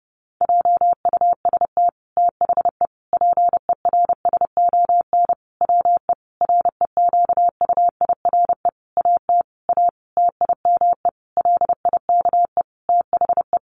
Morse code, heard as a keyed tone, about 30 wpm, 700 Hz. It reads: JUST THE PERSON WE REQUIRE AT A TIME LIKE THE
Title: Sentences from Top 900 Words 30wpm